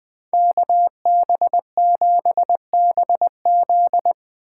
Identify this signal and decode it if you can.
Morse code, heard as a keyed tone, about 20 wpm, 700 Hz. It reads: KB7BZ
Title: Speed Racing - USA 2x2 Call Signs 17wpm